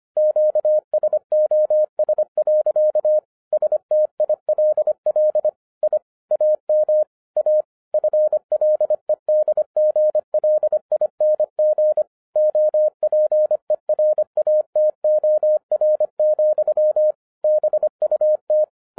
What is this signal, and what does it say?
Morse code, heard as a keyed tone, about 25 wpm, 610 Hz. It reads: QSOS. STILL I AM A FLEDGLING OPERATOR, BUT